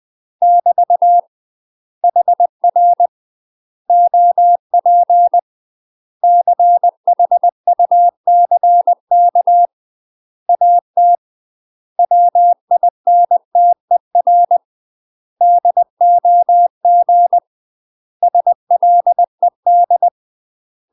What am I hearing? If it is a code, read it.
Morse code, heard as a keyed tone, about 20 wpm, 700 Hz. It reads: <BT> HR OP CHUCK AT WINTER DOG SLED
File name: UK MASTER QSOs TRAINING / 20WPM / UK MASTER QSOs 0036